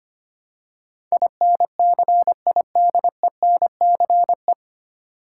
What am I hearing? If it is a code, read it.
Morse code, heard as a keyed tone, about 25 wpm, 700 Hz. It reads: INCIDENCE
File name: practice / 25wpm / 9 Letter Words - First 3 Letters Given - 25wpm